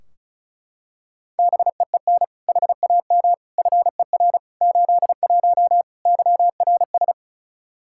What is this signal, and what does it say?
Morse code, heard as a keyed tone, about 36 wpm, 700 Hz. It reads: BEEN HAM FER 81 YRS
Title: Practice QSO - Exchange 3 - More Info - 35wpm